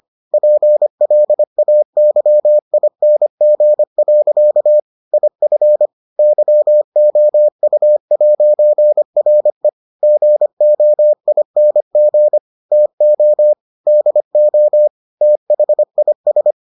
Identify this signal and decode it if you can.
Morse code, heard as a keyed tone, about 25 wpm, 600 Hz. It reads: PLAYING. IF YOU'RE GOING TO DO THIS